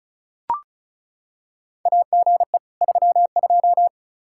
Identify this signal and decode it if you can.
Morse code, heard as a keyed tone, about 34 wpm, 700 Hz. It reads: AGE 32
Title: K1USN SST QSO Practice 35wpm